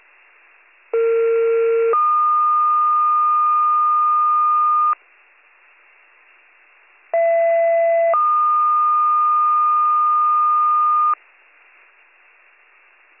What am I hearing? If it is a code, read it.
Two-tone page, heard as a one-second tone, then a three-second tone, about 460 then 1220 Hz, about 670 then 1220 Hz. It is heard from 0.9 to 11.1 s.